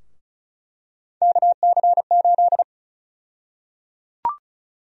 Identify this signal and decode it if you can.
Morse code, heard as a keyed tone, about 34 wpm, 700 Hz. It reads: KC8
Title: USA Call Sign Prefixes - Repeated 2x - 35wpm